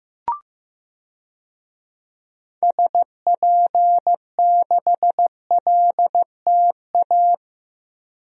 Morse code, heard as a keyed tone, about 15 wpm, 700 Hz. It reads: SP6LTA
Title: DXCC-100 Call Signs - 6 Characters - Repeated 3x - 15wpm